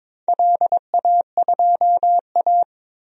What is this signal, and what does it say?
Morse code, heard as a keyed tone, about 22 wpm, 700 Hz. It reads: LA2A
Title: DXCC-100 Call Signs - 4 Characters - Repeated 2x - 22wpm